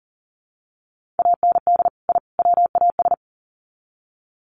Morse code, heard as a keyed tone, about 40 wpm, 700 Hz. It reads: AND I WAS
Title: Sets of 3 Words 40wpm